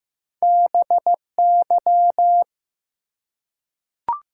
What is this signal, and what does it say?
Morse code, heard as a keyed tone, about 15 wpm, 700 Hz. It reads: BY
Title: Top 180 QSO Elements 15wpm